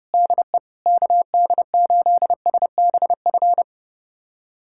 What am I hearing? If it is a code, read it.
Morse code, heard as a keyed tone, about 30 wpm, 690 Hz. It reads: DE KD8SBF
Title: USA Callsign DE Callsign - 6 Characters - 30wpm